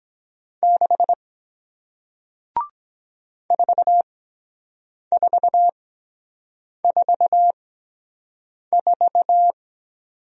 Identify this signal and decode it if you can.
Morse code, heard as a keyed tone, about 22 wpm, spaced as for 7 wpm, 700 Hz. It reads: 6 4444